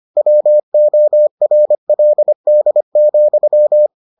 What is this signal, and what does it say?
Morse code, heard as a keyed tone, about 25 wpm, 600 Hz. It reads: WORLD,